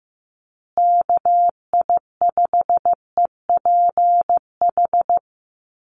Morse code, heard as a keyed tone, about 15 wpm, 700 Hz. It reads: KI5EPH
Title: USA Call Signs Repeated 2x 15wpm